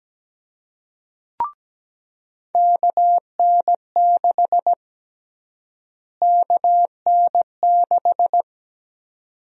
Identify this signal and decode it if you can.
Morse code, heard as a keyed tone, about 17 wpm, 700 Hz. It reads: KN6 KN6
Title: USA Call Sign Prefixes - Repeated 2x - 17wpm